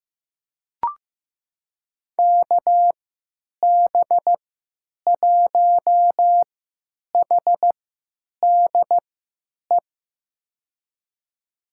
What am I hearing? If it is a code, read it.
Morse code, heard as a keyed tone, about 15 wpm, spaced as for 9 wpm, 700 Hz. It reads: KB1HDE
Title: USA Call Signs - 6 Characters - 3x Character Spacing - 15wpm